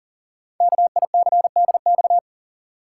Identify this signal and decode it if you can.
Morse code, heard as a keyed tone, about 42 wpm, 690 Hz. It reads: KICDX
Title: Code Groups - 5 Letter:Numbers - 40wpm